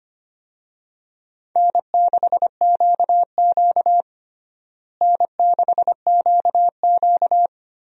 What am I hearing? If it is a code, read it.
Morse code, heard as a keyed tone, about 25 wpm, 700 Hz. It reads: N6QQ N6QQ
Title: USA Call Signs - 4 Characters - Repeated 2x - 25wpm